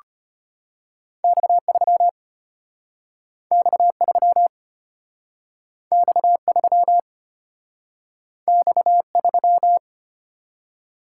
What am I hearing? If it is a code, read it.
Morse code, heard as a keyed tone, about 34 wpm, 700 Hz. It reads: X3 X3 X3 X3